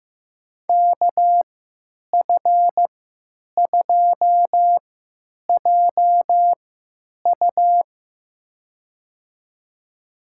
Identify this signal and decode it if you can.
Morse code, heard as a keyed tone, about 16 wpm, spaced as for 9 wpm, 700 Hz. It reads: KF2JU
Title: USA Call Signs - 3x Character Spacing - 15wpm